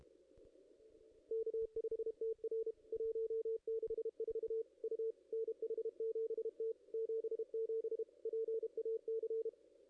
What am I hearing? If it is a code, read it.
Morse code, heard as a keyed tone, about 32 wpm, 440 Hz. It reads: K5TR 164 U NH7T 77 PAC
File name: contests / ss-cw-k5tr-op-n5rz -2200